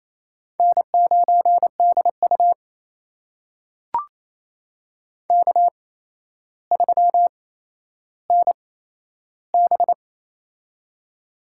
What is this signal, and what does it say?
Morse code, heard as a keyed tone, about 28 wpm, spaced as for 12 wpm, 700 Hz. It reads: N9DU K3NB
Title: USA Call Signs - 4 Characters - 8x Character Spacing - 28wpm